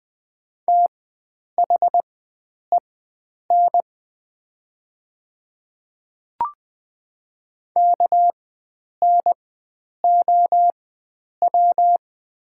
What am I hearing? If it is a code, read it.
Morse code, heard as a keyed tone, about 20 wpm, spaced as for 9 wpm, 700 Hz. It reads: THEN KNOW
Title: 4x Character Spacing - Top 100 Words - 20wpm